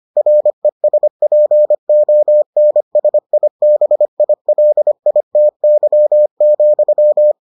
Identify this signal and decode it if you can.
Morse code, heard as a keyed tone, about 25 wpm, 600 Hz. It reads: RESPONSIBILITY,